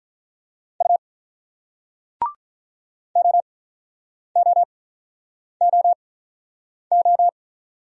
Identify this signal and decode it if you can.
Morse code, heard as a keyed tone, about 44 wpm, spaced as for 9 wpm, 690 Hz. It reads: U OOOO
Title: Speed-Racing Single Letters 35wpm